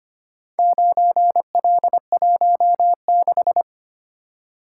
Morse code, heard as a keyed tone, about 25 wpm, 700 Hz. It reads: 9L16
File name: practice / 25wpm / Four Letter-Number Combinations - 25wpm